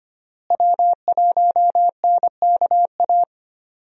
Morse code, heard as a keyed tone, about 25 wpm, 700 Hz. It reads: W1NKA
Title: USA Call Signs - 5 Characters - Repeated 3x - 25wpm